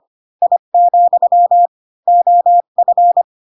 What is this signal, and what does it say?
Morse code, heard as a keyed tone, about 25 wpm, 700 Hz. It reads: I, OF